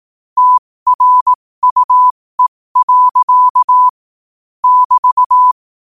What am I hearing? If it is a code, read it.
Morse code, heard as a keyed tone, about 18 wpm, spaced as for 15 wpm, 1000 Hz. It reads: TRUE. <BT>
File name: qotdfiles / 15wpm